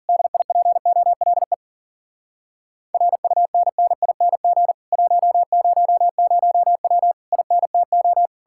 Morse code, heard as a keyed tone, about 52 wpm, 700 Hz. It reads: DIPOLE RUNNING 100W INTO